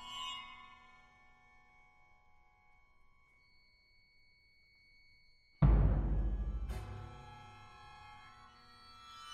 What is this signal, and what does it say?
Dial-up modem handshake.